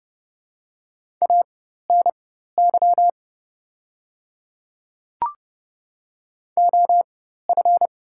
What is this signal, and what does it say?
Morse code, heard as a keyed tone, about 31 wpm, spaced as for 14 wpm, 700 Hz. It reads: ANY OF
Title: 4x Character Spacing - Top 200 Words - 30wpm